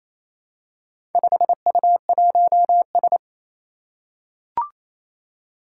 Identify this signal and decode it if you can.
Morse code, heard as a keyed tone, about 29 wpm, 700 Hz. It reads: 5U1S
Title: Four Challenging Letter-Number Combinations 28wpm